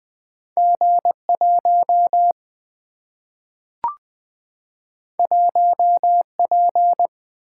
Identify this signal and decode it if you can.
Morse code, heard as a keyed tone, about 20 wpm, 700 Hz. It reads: G 1 1 P
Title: Single-Letter Single-Number 20wpm